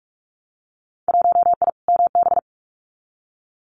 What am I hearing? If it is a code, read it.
Morse code, heard as a keyed tone, about 46 wpm, 700 Hz. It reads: 1I MB
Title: Winter Field Day Exchanges Encore 45wpm